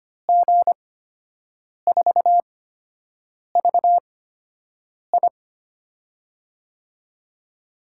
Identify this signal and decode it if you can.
Morse code, heard as a keyed tone, about 25 wpm, spaced as for 7 wpm, 700 Hz. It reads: G4VI
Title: USA Call Signs - 5 Characters - 8x Character Spacing - 25wpm